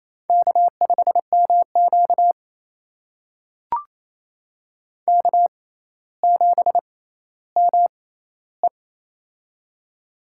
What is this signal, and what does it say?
Morse code, heard as a keyed tone, about 29 wpm, spaced as for 15 wpm, 700 Hz. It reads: K5MQ K7ME
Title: USA Call Signs - 4 Characters - 6x Character Spacing - 28wpm